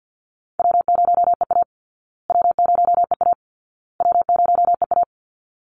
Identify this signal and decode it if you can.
Morse code, heard as a keyed tone, about 50 wpm, 700 Hz. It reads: W0EA W0EA W0EA
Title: Call Signs Repeated 2x - 50wpm